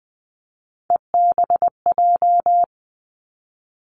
Morse code, heard as a keyed tone, about 20 wpm, 700 Hz. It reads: EBJ
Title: Three Challenging Letter-Number Combinations 20wpm